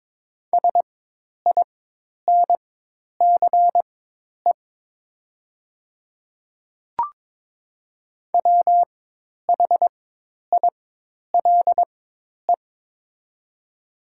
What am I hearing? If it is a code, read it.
Morse code, heard as a keyed tone, about 22 wpm, spaced as for 10 wpm, 700 Hz. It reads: SINCE WHILE
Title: 4x Character Spacing - Top 500 Words - 22wpm